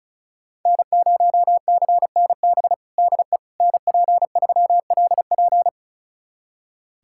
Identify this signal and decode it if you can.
Morse code, heard as a keyed tone, about 36 wpm, 690 Hz. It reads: N0CNB DE NP3LP
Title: USA Callsign DE Callsign - 5 Characters - 35wpm